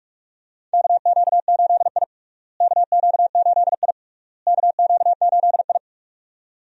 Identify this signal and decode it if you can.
Morse code, heard as a keyed tone, about 46 wpm, 700 Hz. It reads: KQ8I KQ8I KQ8I
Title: USA Call Signs - 4 Characters - Repeated 3x - 45wpm